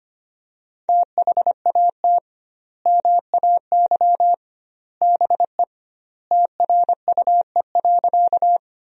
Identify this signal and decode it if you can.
Morse code, heard as a keyed tone, about 25 wpm, 700 Hz. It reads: THAT MAY BE TRUE.
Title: 2x Word-Spacing - Sentences from Top 400 Words - 25wpm